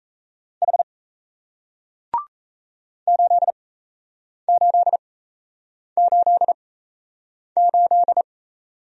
Morse code, heard as a keyed tone, about 38 wpm, spaced as for 8 wpm, 700 Hz. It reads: H 8888